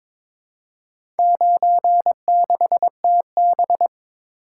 Morse code, heard as a keyed tone, about 22 wpm, 700 Hz. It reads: 96TB
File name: practice / 22wpm / Four Letter-Number Combinations - 22wpm